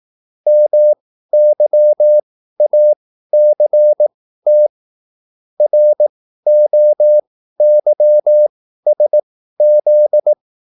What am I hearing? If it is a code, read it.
Morse code, heard as a keyed tone, about 18 wpm, spaced as for 13 wpm, 600 Hz. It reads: MYACT ROYSZ